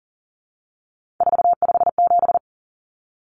Morse code, heard as a keyed tone, about 40 wpm, 700 Hz. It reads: 457
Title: Numbers - 3-Digit 40wpm